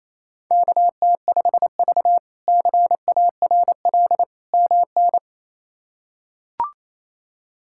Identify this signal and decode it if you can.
Morse code, heard as a keyed tone, about 28 wpm, 700 Hz. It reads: KT5V CARL MN
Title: K1USN SST QSO Practice - 28wpm